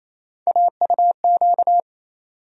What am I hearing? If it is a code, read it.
Morse code, heard as a keyed tone, about 28 wpm, 700 Hz. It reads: AUQ
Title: USA Call Sign Postfixes - 3 Letters 28wpm